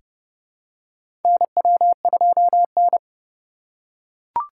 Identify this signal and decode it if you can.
Morse code, heard as a keyed tone, about 32 wpm, 700 Hz. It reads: NW2N
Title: USA Call Signs - 4 Characters - 2x Character Spacing - 30wpm